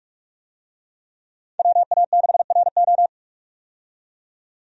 Morse code, heard as a keyed tone, about 46 wpm, 700 Hz. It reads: WA6RO
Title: USA Call Signs - 5 Characters - 45wpm